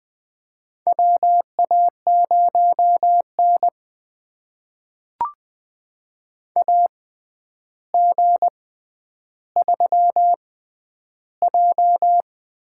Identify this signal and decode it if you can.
Morse code, heard as a keyed tone, about 21 wpm, spaced as for 10 wpm, 700 Hz. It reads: WA0N AG3J
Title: USA Call Signs - 4 Characters - 6x Character Spacing - 20wpm